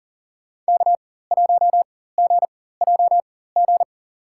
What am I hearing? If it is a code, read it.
Morse code, heard as a keyed tone, about 40 wpm, spaced as for 19 wpm, 700 Hz. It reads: K1GJG
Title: USA Call Signs - 5 Characters - 4x Character Spacing - 40wpm